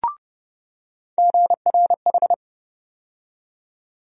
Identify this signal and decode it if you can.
Morse code, heard as a keyed tone, about 32 wpm, 700 Hz. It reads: GRH
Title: USA Call Sign Postfixes - 3 Letters 30wpm